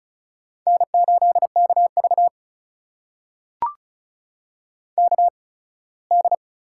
Morse code, heard as a keyed tone, about 36 wpm, spaced as for 21 wpm, 700 Hz. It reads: N8KV KD
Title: USA Call Signs - 8x Character Spacing - 35wpm